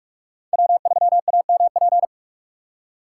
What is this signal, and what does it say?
Morse code, heard as a keyed tone, about 44 wpm, 700 Hz. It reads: W3AMP